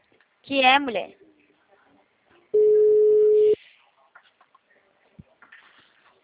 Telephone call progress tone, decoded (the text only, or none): ringback tone